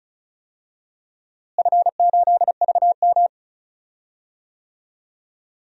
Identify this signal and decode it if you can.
Morse code, heard as a keyed tone, about 34 wpm, 700 Hz. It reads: F8VM
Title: DXCC-100 Call Signs - 4 Characters - Repeated 3x - 35wpm